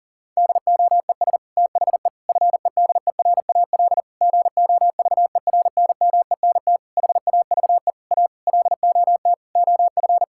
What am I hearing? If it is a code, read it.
Morse code, heard as a keyed tone, about 40 wpm, 700 Hz. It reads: DOES THE FEDERAL GOVERNMENT HAVE A LOT OF
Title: Sentences from Top 1000 Words - Review Part 5 - 40wpm